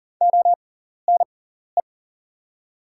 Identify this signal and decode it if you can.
Morse code, heard as a keyed tone, about 38 wpm, spaced as for 14 wpm, 700 Hz. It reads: ONE
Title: 6x Character Spacing - Top 200 Words - 40wpm